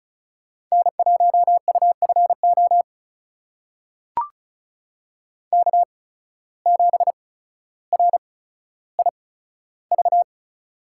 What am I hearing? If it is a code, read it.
Morse code, heard as a keyed tone, about 36 wpm, spaced as for 15 wpm, 700 Hz. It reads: N1UFO K7RIV
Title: USA Call Signs - 8x Character Spacing - 35wpm